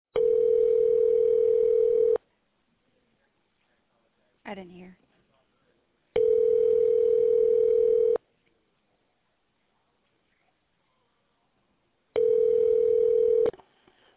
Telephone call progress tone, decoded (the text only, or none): ringback tone